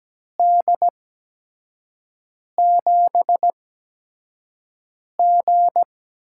Morse code, heard as a keyed tone, about 17 wpm, spaced as for 5 wpm, 700 Hz. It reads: D7G
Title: USA Call Signs - 8x Character Spacing - 17wpm